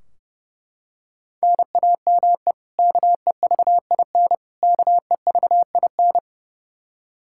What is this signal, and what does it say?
Morse code, heard as a keyed tone, about 30 wpm, 700 Hz. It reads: NAME KEVIN KEVIN
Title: Practice QSO - Exchange 1 - Beginning - 30wpm